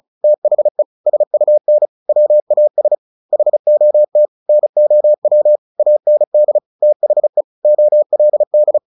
Morse code, heard as a keyed tone, about 34 wpm, 600 Hz. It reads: THE SUN WAS HOT NOW AND THE OLD
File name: the old man and the sea / the old man and the sea 1 35wpm